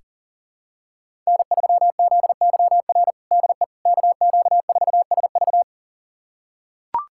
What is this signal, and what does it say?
Morse code, heard as a keyed tone, about 40 wpm, 700 Hz. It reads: N3ZYR DE KQ4SV
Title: USA Callsign DE Callsign - 5 Characters - 40wpm